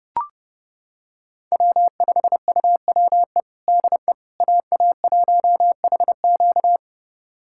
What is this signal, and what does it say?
Morse code, heard as a keyed tone, about 30 wpm, 700 Hz. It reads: W5UWE DE AA1HQ
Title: USA Callsign DE Callsign - 5 Characters - 30wpm